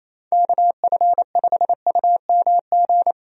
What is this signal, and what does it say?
Morse code, heard as a keyed tone, about 28 wpm, 700 Hz. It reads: KF5UMG
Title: USA Call Signs - 6 Characters - 28wpm